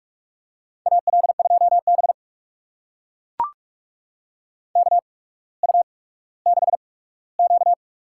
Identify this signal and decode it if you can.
Morse code, heard as a keyed tone, about 44 wpm, spaced as for 19 wpm, 700 Hz. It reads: AL2B KU6Q
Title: USA Call Signs - 4 Characters - 8x Character Spacing - 45wpm